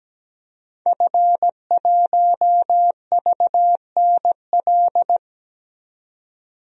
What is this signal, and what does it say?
Morse code, heard as a keyed tone, about 17 wpm, 700 Hz. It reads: F1VNL
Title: DXCC-100 Call Signs - 5 Characters - 17wpm